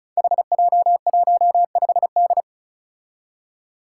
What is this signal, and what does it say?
Morse code, heard as a keyed tone, about 36 wpm, 700 Hz. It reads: HJ15D
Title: Five Challenging Letter-Number Combinations 35wpm